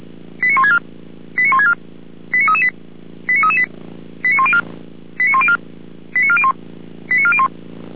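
Five-tone selective call (ZVEI), heard as quick runs of five tones, tones of about 70 ms. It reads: 89164 89164 89338 89338 89104 89104 89401 89401
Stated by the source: Five-tone ZVEI selective call 89701 received by SDR.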